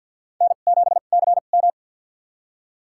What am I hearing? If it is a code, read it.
Morse code, heard as a keyed tone, about 46 wpm, spaced as for 34 wpm, 700 Hz. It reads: N7CM
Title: USA Call Signs - 4 Characters - 2x Character Spacing - 50wpm